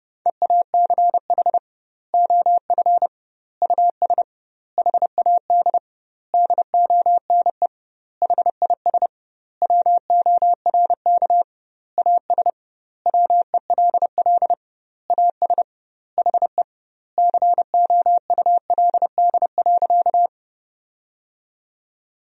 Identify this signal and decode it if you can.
Morse code, heard as a keyed tone, about 30 wpm, 700 Hz. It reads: EACH OF US HAD DONE HIS WORK AS WELL AS HE COULD.
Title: 2x Word-Spacing - Sentences from Top 500 Words - 30wpm